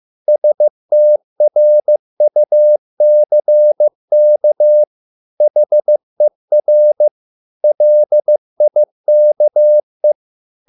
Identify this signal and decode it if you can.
Morse code, heard as a keyed tone, about 15 wpm, 600 Hz. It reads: STRUCK HER LIKE